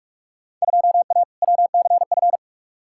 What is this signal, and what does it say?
Morse code, heard as a keyed tone, about 44 wpm, 700 Hz. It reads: 2A WCF
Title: Field Day Exchanges Encore - 45wpm